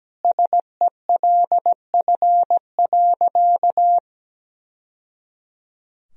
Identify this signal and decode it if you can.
Morse code, heard as a keyed tone, about 17 wpm, 700 Hz. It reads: SELF.
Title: Sentences from Top 1000 Words - Review Part 1 - 17wpm